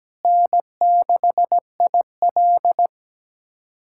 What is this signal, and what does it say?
Morse code, heard as a keyed tone, about 17 wpm, 700 Hz. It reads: N6IL
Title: USA Callsign DE Callsign - 4 Characters - 17wpm